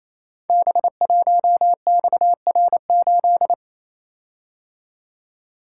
Morse code, heard as a keyed tone, about 28 wpm, 700 Hz. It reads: B1XR8
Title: Code Groups - 5 Letter:Numbers - 28wpm